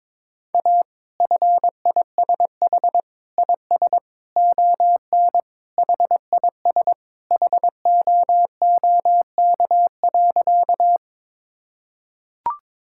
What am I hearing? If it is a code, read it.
Morse code, heard as a keyed tone, about 22 wpm, 700 Hz. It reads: A FISH IS ON HIS HOOK.
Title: Sentences from Top 1600 Words 22wpm